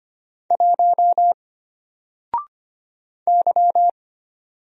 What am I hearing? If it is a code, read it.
Morse code, heard as a keyed tone, about 25 wpm, 700 Hz. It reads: 1 Y
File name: Single-Letter-Number-25wpm